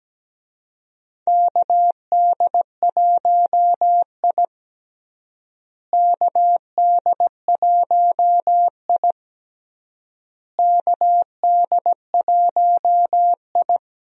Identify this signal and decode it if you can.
Morse code, heard as a keyed tone, about 17 wpm, 700 Hz. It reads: KD1I KD1I KD1I
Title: USA Call Signs - 4 Characters - Repeated 3x - 17wpm